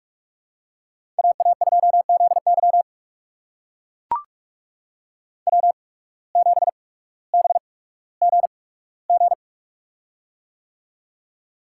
Morse code, heard as a keyed tone, about 42 wpm, spaced as for 19 wpm, 700 Hz. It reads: AA2ZY W7BGG
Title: USA Call Signs - 5 Characters - 8x Character Spacing - 45wpm